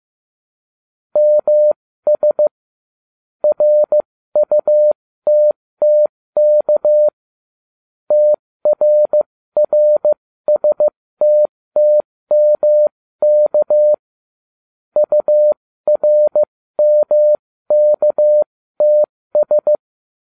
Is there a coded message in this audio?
Yes: Morse code from 1.2 to 19.8 s, heard as a keyed tone, about 15 wpm, spaced as for 13 wpm, 600 Hz. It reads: MS RUTTK TRRSTTMK URMKTS